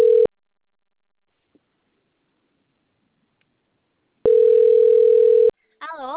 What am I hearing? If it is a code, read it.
Ringback tone (North America), 440 + 480 Hz, with speech.
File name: rg-1600-unknown- - - .45184